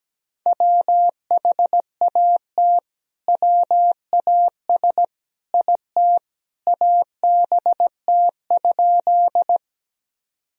Morse code, heard as a keyed tone, about 17 wpm, 700 Hz. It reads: WHAT WAS IT ABT?
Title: Sentences from Top 100 Words and Abbreviations 17wpm